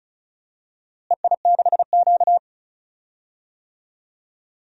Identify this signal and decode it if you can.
Morse code, heard as a keyed tone, about 34 wpm, 700 Hz. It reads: EI6Q
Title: DXCC-100 Call Signs - 4 Characters - Repeated 3x - 35wpm